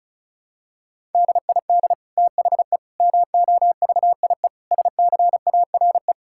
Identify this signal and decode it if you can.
Morse code, heard as a keyed tone, about 36 wpm, 700 Hz. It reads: DID THE MOVIE SCARE
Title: Sentences from Top 1000 Words - Review Part 5 - 35wpm